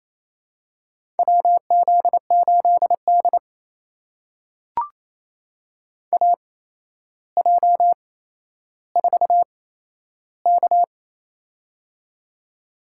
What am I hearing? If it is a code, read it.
Morse code, heard as a keyed tone, about 28 wpm, spaced as for 12 wpm, 700 Hz. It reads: WZ8D AJ4K
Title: USA Call Signs - 8x Character Spacing - 28wpm